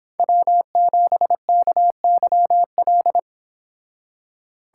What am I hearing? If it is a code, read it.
Morse code, heard as a keyed tone, about 26 wpm, 700 Hz. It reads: W7KYL
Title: Speed Racing - USA 1x3 Call Signs 22wpm